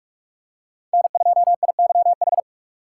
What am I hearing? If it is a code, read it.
Morse code, heard as a keyed tone, about 46 wpm, 700 Hz. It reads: N2IYH
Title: USA Call Signs - 5 Characters - 4x Character Spacing - 45wpm